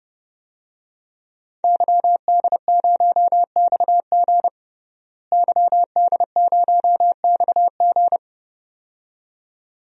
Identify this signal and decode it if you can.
Morse code, heard as a keyed tone, about 30 wpm, 700 Hz. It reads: YD0XG YD0XG
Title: DXCC-100 Call Signs - 5 Characters - Repeated 2x - 30wpm